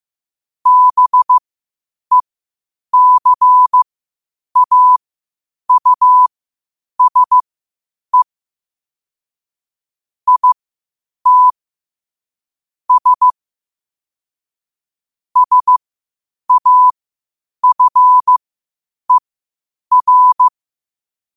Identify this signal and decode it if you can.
Morse code, heard as a keyed tone, about 15 wpm, spaced as for 8 wpm, 1000 Hz. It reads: BECAUSE IT S SAFER
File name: qotdfiles / 8wpm